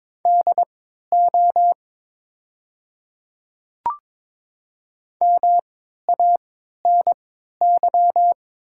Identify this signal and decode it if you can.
Morse code, heard as a keyed tone, about 22 wpm, spaced as for 12 wpm, 700 Hz. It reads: DO MANY
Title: 3x Character Spacing - Top 300 Words - 22wpm